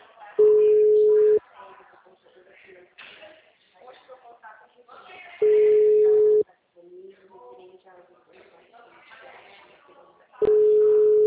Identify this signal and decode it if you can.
Ringback tone (Europe, 425 Hz), with speech.
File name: out- -2009- - - .93102